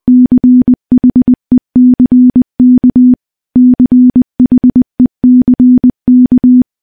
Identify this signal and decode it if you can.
Morse code, heard as a keyed tone, about 20 wpm, 250 Hz. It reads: CHECK CHECK
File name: morse / morse-training-20WPM